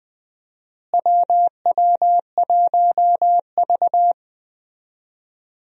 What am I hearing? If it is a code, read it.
Morse code, heard as a keyed tone, about 20 wpm, 700 Hz. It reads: WW1V
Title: USA Call Signs - 4 Characters - Repeated 2x - 20wpm